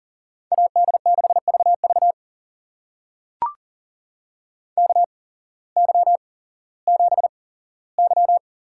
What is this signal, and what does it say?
Morse code, heard as a keyed tone, about 40 wpm, spaced as for 19 wpm, 700 Hz. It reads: AD6VV KY7Y
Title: USA Call Signs - 8x Character Spacing - 40wpm